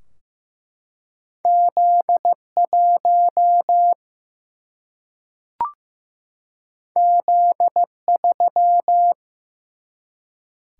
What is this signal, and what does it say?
Morse code, heard as a keyed tone, about 15 wpm, 700 Hz. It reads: Z 1 Z 3
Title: Single-Letter Single-Number 15wpm